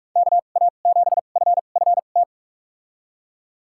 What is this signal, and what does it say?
Morse code, heard as a keyed tone, about 41 wpm, spaced as for 31 wpm, 700 Hz. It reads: KA7FFT
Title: USA Call Signs - 6 Characters - 2x Character Spacing - 45wpm